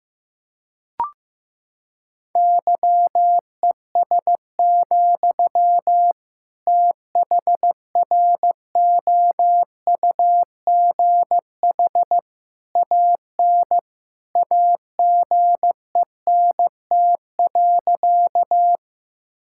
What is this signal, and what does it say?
Morse code, heard as a keyed tone, about 16 wpm, 700 Hz. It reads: YES, THROUGH AN AGENT.
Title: Sentences from Top 1600 Words 15wpm